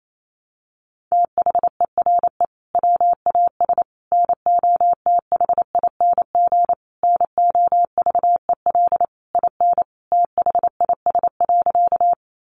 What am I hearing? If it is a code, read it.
Morse code, heard as a keyed tone, about 28 wpm, 700 Hz. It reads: THERE WAS NOTHING NOVEL IN THIS.
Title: Sentences from Top 1300 Words 28wpm